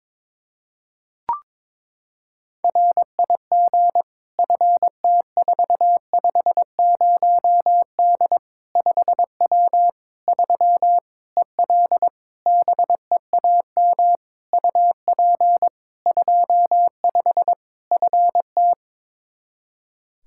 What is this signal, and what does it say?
Morse code, heard as a keyed tone, about 22 wpm, 700 Hz. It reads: RIG FT450D 5W 3 EL BEAM UP 25 FT
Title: Practice QSO - Exchange 2 - Info - 22wpm